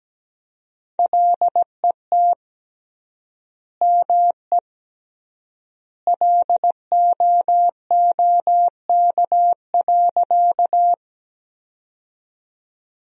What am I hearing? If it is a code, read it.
Morse code, heard as a keyed tone, about 17 wpm, 700 Hz. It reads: LET ME LOOK.
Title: 3x Word-Spacing - Sentences from Top 300 Words - 17wpm